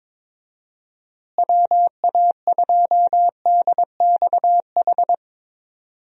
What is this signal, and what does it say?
Morse code, heard as a keyed tone, about 22 wpm, 700 Hz. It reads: WA2DXH
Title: USA Call Signs - 6 Characters - Repeated 2x - 22wpm